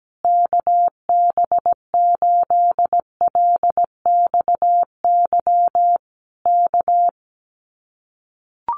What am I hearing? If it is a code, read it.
Morse code, heard as a keyed tone, about 17 wpm, 700 Hz. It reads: KB8LXY K
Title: Common QSO Phrases Encore 17wpm